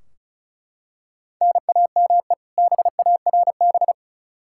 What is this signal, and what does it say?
Morse code, heard as a keyed tone, about 36 wpm, 700 Hz. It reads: NAME BARB
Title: K1USN SST QSO Practice 35wpm